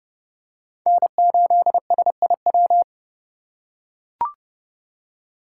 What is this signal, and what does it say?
Morse code, heard as a keyed tone, about 32 wpm, 700 Hz. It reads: N8SIW
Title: USA Call Signs - 5 Characters - 3x Character Spacing - 30wpm